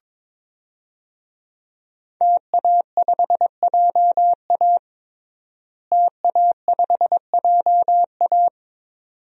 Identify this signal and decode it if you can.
Morse code, heard as a keyed tone, about 22 wpm, 700 Hz. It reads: TA5JA TA5JA
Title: DXCC-100 Call Signs - 5 Characters - Repeated 2x - 22wpm